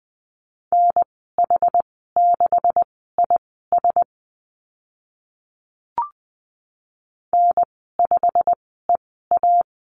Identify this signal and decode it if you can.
Morse code, heard as a keyed tone, about 20 wpm, spaced as for 14 wpm, 700 Hz. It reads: NH6IS N5EA
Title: 154 Lesson 25g - Letter p - Call Signs 20wpm